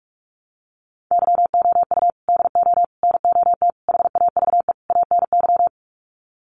Morse code, heard as a keyed tone, about 44 wpm, 700 Hz. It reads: YOU DO NOT HAVE ANY
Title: Sets of 5 Words Encore 45wpm